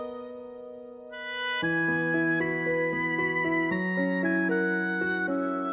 Emergency Alert System (SAME) tones, heard as harsh buzzing bursts. It is heard from 1.1 to 5.2 s.